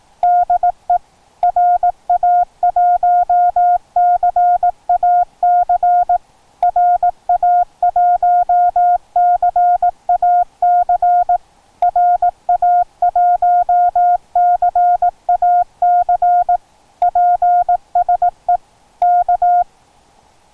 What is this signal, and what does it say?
Morse code, heard as a keyed tone, about 18 wpm, 710 Hz. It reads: DE RA1CAC RA1CAC RA1CAC PSE K